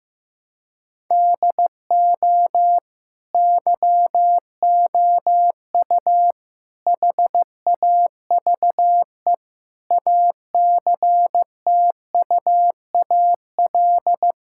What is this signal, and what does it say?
Morse code, heard as a keyed tone, about 15 wpm, 700 Hz. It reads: DO YOU HAVE ACTUAL
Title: Sentences from Top 1000 Words - Review Part 5 - 15wpm